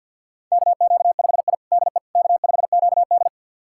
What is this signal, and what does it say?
Morse code, heard as a keyed tone, about 47 wpm, 700 Hz. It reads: KQ5I DE K5QD
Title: USA Callsign DE Callsign - 4 Characters - 50wpm